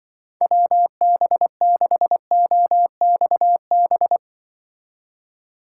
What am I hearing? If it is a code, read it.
Morse code, heard as a keyed tone, about 24 wpm, 700 Hz. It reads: WB6OXB
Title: Speed Racing - USA 2x3 Call Signs 20wpm